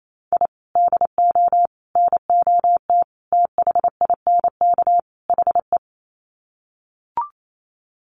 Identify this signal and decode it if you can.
Morse code, heard as a keyed tone, about 28 wpm, 700 Hz. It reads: I DO NOT THINK HE
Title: Sets of 5 Words 28wpm